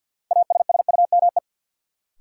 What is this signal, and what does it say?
Morse code, heard as a keyed tone, about 51 wpm, 700 Hz. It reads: ASSUME
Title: Sentences from Top 1500 Words - New Word Drill Alt - 50wpm